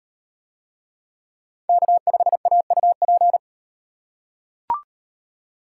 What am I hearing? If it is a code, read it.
Morse code, heard as a keyed tone, about 36 wpm, 700 Hz. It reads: K5AUP